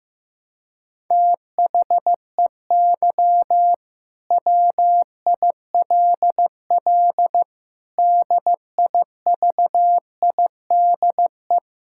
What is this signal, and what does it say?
Morse code, heard as a keyed tone, about 16 wpm, 700 Hz. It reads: THEY WILL DIVIDE
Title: Sentences from Top 1600 Words 15wpm